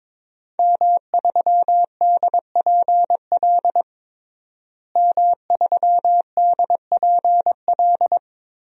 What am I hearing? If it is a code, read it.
Morse code, heard as a keyed tone, about 22 wpm, 700 Hz. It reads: M3DPL M3DPL
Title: DXCC-100 Call Signs - 5 Characters - Repeated 2x - 22wpm